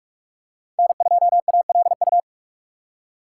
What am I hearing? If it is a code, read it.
Morse code, heard as a keyed tone, about 46 wpm, 700 Hz. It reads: N2ALU
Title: USA Call Signs - 5 Characters - 6x Character Spacing - 45wpm